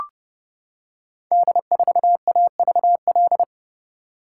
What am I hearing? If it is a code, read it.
Morse code, heard as a keyed tone, about 30 wpm, 700 Hz. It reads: D4AVL
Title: Code Groups - 5 Letter:Numbers - 30wpm